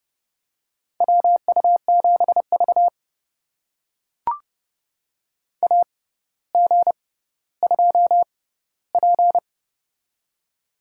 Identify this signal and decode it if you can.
Morse code, heard as a keyed tone, about 30 wpm, spaced as for 15 wpm, 700 Hz. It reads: WU7V AG2P
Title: USA Call Signs - 4 Characters - 6x Character Spacing - 30wpm